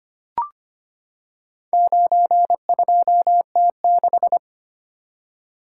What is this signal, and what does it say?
Morse code, heard as a keyed tone, about 25 wpm, 700 Hz. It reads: 92T6
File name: practice / 25wpm / Four Letter-Number Combinations - 25wpm